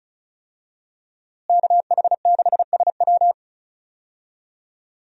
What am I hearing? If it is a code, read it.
Morse code, heard as a keyed tone, about 35 wpm, 700 Hz. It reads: KH6SW
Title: DXCC-100 Call Signs - 5 Characters - Repeated 2x - 35wpm